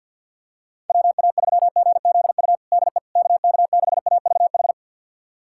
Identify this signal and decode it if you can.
Morse code, heard as a keyed tone, about 50 wpm, 700 Hz. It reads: WA3GZU DE KK6AVH
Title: USA Callsign DE Callsign - 6 Characters - 50wpm